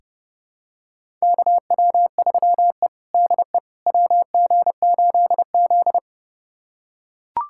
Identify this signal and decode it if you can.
Morse code, heard as a keyed tone, about 30 wpm, 700 Hz. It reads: KW3E DE WG8Z
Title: USA Callsign DE Callsign - 4 Characters - 30wpm